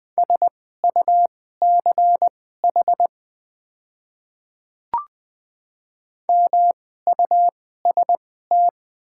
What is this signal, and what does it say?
Morse code, heard as a keyed tone, about 20 wpm, spaced as for 14 wpm, 700 Hz. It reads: SUCH MUST
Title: 2x Character Spacing - Top 500 Words - 20wpm